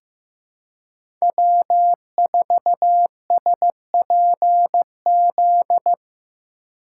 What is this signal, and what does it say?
Morse code, heard as a keyed tone, about 16 wpm, 700 Hz. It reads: W4SPZ
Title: USA Call Signs - 4x Character Spacing - 15wpm